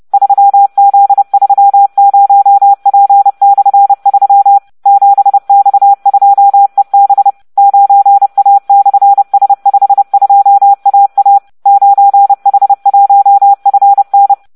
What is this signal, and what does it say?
Morse code, heard as a keyed tone, about 30 wpm, 800 Hz. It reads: 3Z30P/3 7X2EB 9A/S52AA 9H1FN